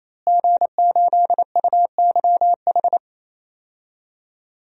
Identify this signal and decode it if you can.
Morse code, heard as a keyed tone, about 28 wpm, 700 Hz. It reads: G8UYH
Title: DXCC-100 Call Signs - 5 Characters - Repeated 3x - 28wpm